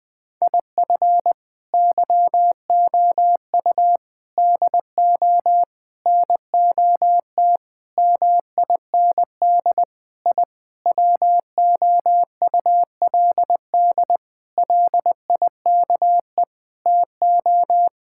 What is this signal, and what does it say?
Morse code, heard as a keyed tone, about 20 wpm, 700 Hz. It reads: IF YOU DO NOT MIND I WOULD LIKE TO
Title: Sentences with 5-Word Phrases - Spoken Beginning - 20wpm